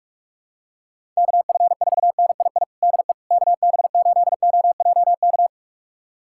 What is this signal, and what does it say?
Morse code, heard as a keyed tone, about 46 wpm, 700 Hz. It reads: KF4NII DE KB8OJK
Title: USA Callsign DE Callsign - 6 Characters - 45wpm